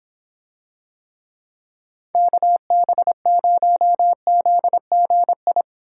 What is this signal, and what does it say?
Morse code, heard as a keyed tone, about 25 wpm, 700 Hz. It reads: KB0ZGI